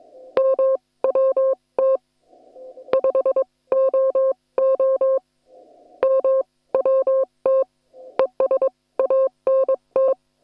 Morse code, heard as a keyed tone, about 22 wpm, spaced as for 18 wpm, 530 Hz. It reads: MWT 5OO MWT ES ANT
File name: attachments / hf 1 001